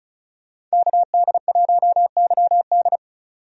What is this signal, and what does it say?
Morse code, heard as a keyed tone, about 36 wpm, 700 Hz. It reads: KD1YD
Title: USA Call Signs - 5 Characters - 35wpm